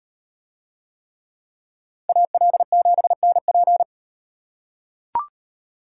Morse code, heard as a keyed tone, about 36 wpm, 700 Hz. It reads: AL7NP